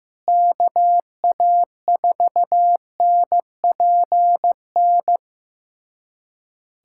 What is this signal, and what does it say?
Morse code, heard as a keyed tone, about 15 wpm, 700 Hz. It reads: KA4NPN
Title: USA Callsign DE Callsign - 6 Characters - 15wpm